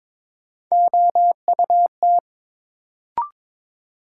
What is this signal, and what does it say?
Morse code, heard as a keyed tone, about 21 wpm, 700 Hz. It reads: OUT
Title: Sentences from Top 100 Words - New Word Drill - 22wpm